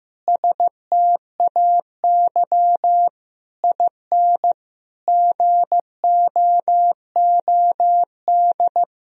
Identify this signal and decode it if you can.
Morse code, heard as a keyed tone, about 15 wpm, 700 Hz. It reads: STAY IN GOOD
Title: Sentences from Top 1000 Words - Review Part 3 - 15wpm